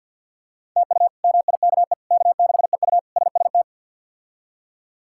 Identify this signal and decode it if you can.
Morse code, heard as a keyed tone, about 46 wpm, 700 Hz. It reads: TU MIKE K6EU SST
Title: K1USN SST QSO Practice - 50wpm